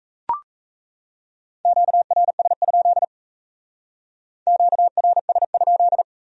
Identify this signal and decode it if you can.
Morse code, heard as a keyed tone, about 35 wpm, 700 Hz. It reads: QRS? QRS?